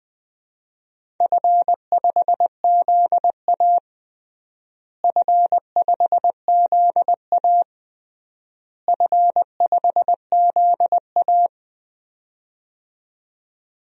Morse code, heard as a keyed tone, about 20 wpm, 700 Hz. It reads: F5ZA F5ZA F5ZA
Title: DXCC-100 Call Signs - 4 Characters - Repeated 3x - 20wpm